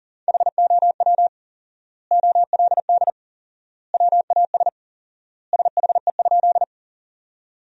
Morse code, heard as a keyed tone, about 40 wpm, 700 Hz. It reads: HOW OLD WAS SHE?
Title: 4x Word-Spacing - Sentences from Top 200 Words - 40wpm